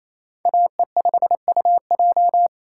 Morse code, heard as a keyed tone, about 35 wpm, 700 Hz. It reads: AE5UJ
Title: Speed Racing - USA 2x2 Call Signs 28wpm